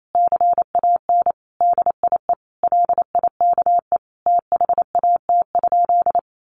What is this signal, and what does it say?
Morse code, heard as a keyed tone, about 28 wpm, 700 Hz. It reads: CAN DIE LIKE THAT?
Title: Sentences from Top 1400 Words 28wpm